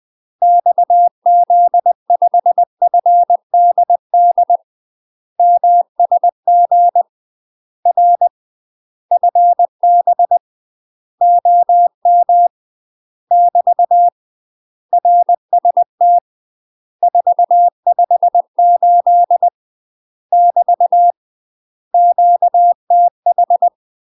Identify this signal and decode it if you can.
Morse code, heard as a keyed tone, about 20 wpm, 700 Hz. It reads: XZ5FDD MSG R FB OM <BT> RST 458 <BT> QTH
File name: UK MASTER QSOs TRAINING / 20WPM / UK MASTER QSOs 0072